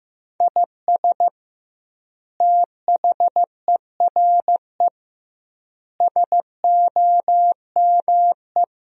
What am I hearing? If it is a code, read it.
Morse code, heard as a keyed tone, about 15 wpm, 700 Hz. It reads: IS THERE SOME
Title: 2x Word-Spacing - Sentences from Top 200 Words - 15wpm